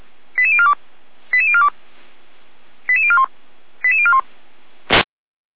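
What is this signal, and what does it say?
Five-tone selective call (ZVEI), heard as quick runs of five tones, tones of about 70 ms. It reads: 88042 88042 88041 88041
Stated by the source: Received on an SDR, ZVEI five-tone code 88999.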